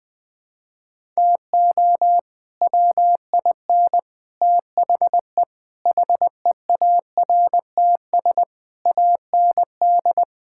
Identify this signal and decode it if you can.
Morse code, heard as a keyed tone, about 21 wpm, 700 Hz. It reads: TO WIN THE HEARTS AND
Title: Sets of 8 Words - 20wpm